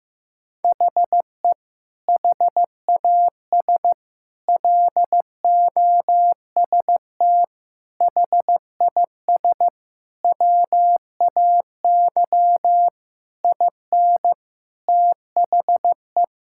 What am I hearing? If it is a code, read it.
Morse code, heard as a keyed tone, about 15 wpm, 700 Hz. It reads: HE HAS LOST HIS WAY IN THE
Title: Sentences from Top 1600 Words 15wpm